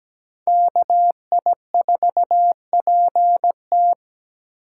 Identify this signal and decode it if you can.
Morse code, heard as a keyed tone, about 20 wpm, 700 Hz. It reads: KI4PT
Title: Speed Racing - USA 2x2 Call Signs 17wpm